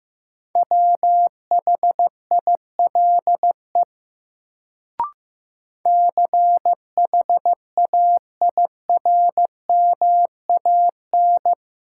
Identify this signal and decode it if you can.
Morse code, heard as a keyed tone, about 15 wpm, 700 Hz. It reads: WHILE CHAIRMAN
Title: Sets of 3 Words - New Word Drill - 15wpm